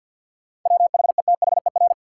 Morse code, heard as a keyed tone, about 53 wpm, 700 Hz. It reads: WHETHER
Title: Sets of 3 Words - New Word Drill - 50wpm